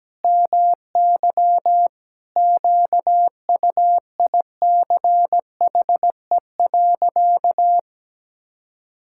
Morse code, heard as a keyed tone, about 17 wpm, 700 Hz. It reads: MY QUICHE.
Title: Pangram Challenge v2 - 17wpm